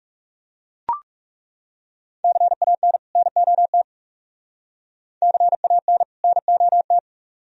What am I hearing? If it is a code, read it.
Morse code, heard as a keyed tone, about 38 wpm, 700 Hz. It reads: CAN NOT CAN NOT